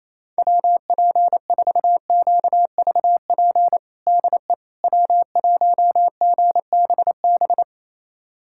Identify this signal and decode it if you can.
Morse code, heard as a keyed tone, about 28 wpm, 700 Hz. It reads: WP4QVP DE W1GBB
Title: USA Callsign DE Callsign 28wpm